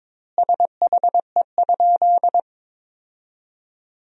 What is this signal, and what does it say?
Morse code, heard as a keyed tone, about 22 wpm, 700 Hz. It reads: SHE?
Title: 2x Word-Spacing - Sentences from Top 100 Words - 22wpm